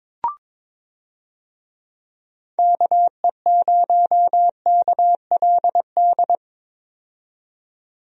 Morse code, heard as a keyed tone, about 22 wpm, 700 Hz. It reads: KE0KLD
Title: USA Call Signs - 6 Characters - 22wpm